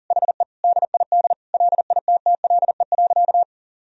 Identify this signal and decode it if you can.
Morse code, heard as a keyed tone, about 40 wpm, 700 Hz. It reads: HE DID LITTLE.
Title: Sentences from Top 1000 Words - Review Part 1 - 40wpm